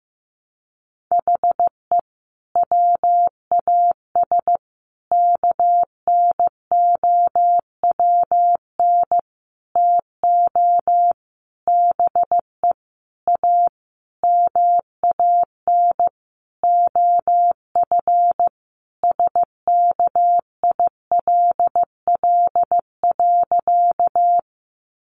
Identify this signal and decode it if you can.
Morse code, heard as a keyed tone, about 15 wpm, 700 Hz. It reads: HE WAS KNOWN TO BE A MAN OF SKILL.
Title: Sentences from Top 1200 Words 15wpm